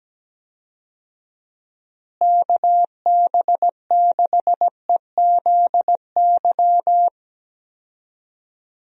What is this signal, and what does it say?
Morse code, heard as a keyed tone, about 17 wpm, 700 Hz. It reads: KB6EZY
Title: USA Call Signs - 6 Characters - Repeated 3x - 17wpm